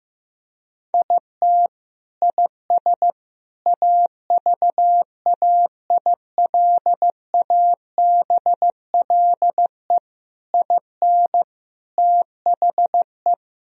Morse code, heard as a keyed tone, about 15 wpm, 700 Hz. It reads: IT IS AVAILABLE IN THE
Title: Sentences from Top 1000 Words - Review Part 3 - 15wpm